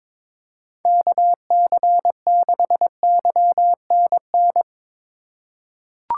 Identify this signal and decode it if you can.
Morse code, heard as a keyed tone, about 22 wpm, 700 Hz. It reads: KC6YNN
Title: USA Call Signs - 6 Characters - 3x Character Spacing - 22wpm